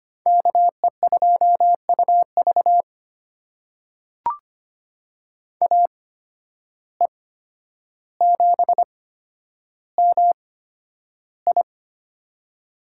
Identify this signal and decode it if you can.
Morse code, heard as a keyed tone, about 25 wpm, spaced as for 11 wpm, 700 Hz. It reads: KE2UV AE7MI
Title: USA Call Signs - 5 Characters - 8x Character Spacing - 25wpm